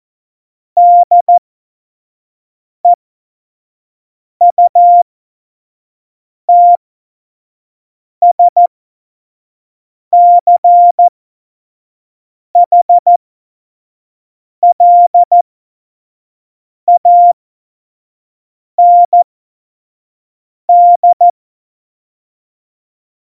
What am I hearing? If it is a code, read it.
Morse code, heard as a keyed tone, about 14 wpm, spaced as for 5 wpm, 700 Hz. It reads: DEUTSCHLAND
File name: Morsekurs / Witze / Witz 30. Juli 2019, 25 F70